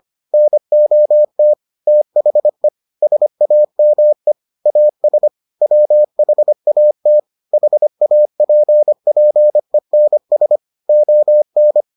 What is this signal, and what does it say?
Morse code, heard as a keyed tone, about 25 wpm, 600 Hz. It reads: NOT THE SAME AS WHAT HAPPENS ON